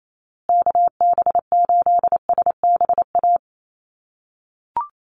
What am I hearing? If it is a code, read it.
Morse code, heard as a keyed tone, about 28 wpm, 700 Hz. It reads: KB8SBA
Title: USA Call Signs Repeated 3x 28wpm